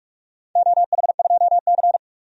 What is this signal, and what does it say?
Morse code, heard as a keyed tone, about 44 wpm, 700 Hz. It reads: OH2C
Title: DXCC-100 Call Signs - 4 Characters - 45wpm